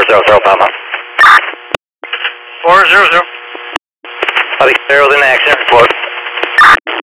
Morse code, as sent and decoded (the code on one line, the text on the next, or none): none